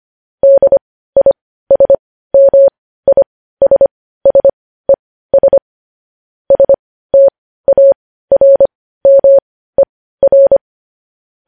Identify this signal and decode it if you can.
Morse code, heard as a keyed tone, about 25 wpm, spaced as for 15 wpm, 550 Hz. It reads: DISMISSES STARMER
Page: News Headlines in Morse Code at 15 WPM